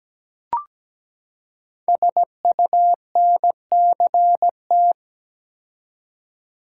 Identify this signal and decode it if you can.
Morse code, heard as a keyed tone, about 17 wpm, 700 Hz. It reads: SUNCT
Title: Code Groups - 5 Letter:Numbers - 17wpm